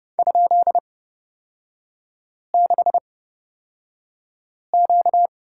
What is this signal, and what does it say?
Morse code, heard as a keyed tone, about 29 wpm, spaced as for 5 wpm, 690 Hz. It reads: ?6Q
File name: Lesson 38e - Letter q - Three Characters 30wpm